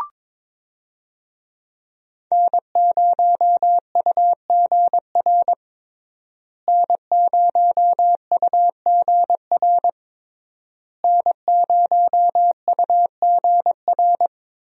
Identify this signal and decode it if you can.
Morse code, heard as a keyed tone, about 22 wpm, 700 Hz. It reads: N0UGR N0UGR N0UGR
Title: USA Call Signs - 5 Characters - Repeated 3x - 22wpm